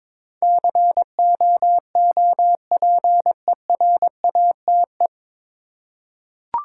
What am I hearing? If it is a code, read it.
Morse code, heard as a keyed tone, about 22 wpm, 700 Hz. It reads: COOPERATE
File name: 9 Letter Words - First 3 Letters Given - 22wpm